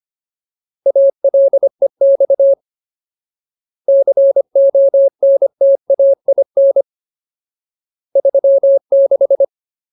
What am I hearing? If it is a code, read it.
Morse code, heard as a keyed tone, about 25 wpm, 550 Hz. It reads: ALEX CONTAIN 36
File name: Ultimate Word List / 25WPM Ultimate